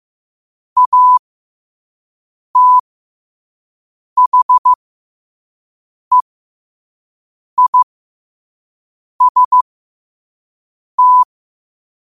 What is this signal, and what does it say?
Morse code, heard as a keyed tone, about 15 wpm, spaced as for 5 wpm, 1000 Hz. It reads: ATHEIST
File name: qotdfiles / 5wpm